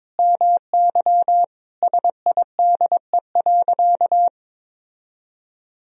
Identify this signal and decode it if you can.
Morse code, heard as a keyed tone, about 22 wpm, 700 Hz. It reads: MY SIDE.
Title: Sentences from Top 1000 Words - Review Part 1 - 22wpm